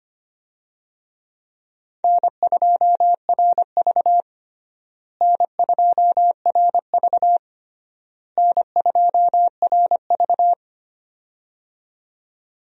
Morse code, heard as a keyed tone, about 25 wpm, 700 Hz. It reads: N2RV N2RV N2RV
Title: USA Call Signs - 4 Characters - Repeated 3x - 25wpm